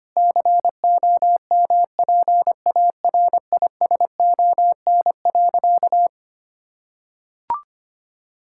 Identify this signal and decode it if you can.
Morse code, heard as a keyed tone, about 25 wpm, 700 Hz. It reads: COMPARISON.
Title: Sentences from Top 1600 Words 25wpm